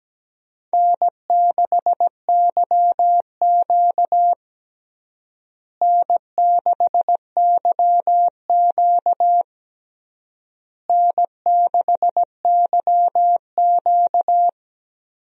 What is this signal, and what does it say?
Morse code, heard as a keyed tone, about 17 wpm, 700 Hz. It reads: N6YQ N6YQ N6YQ
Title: USA Call Signs - 4 Characters - Repeated 3x - 17wpm